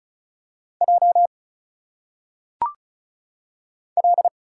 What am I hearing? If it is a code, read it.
Morse code, heard as a keyed tone, about 35 wpm, 700 Hz. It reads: J L
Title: Single Letters - Phonetic Pronunciation 35wpm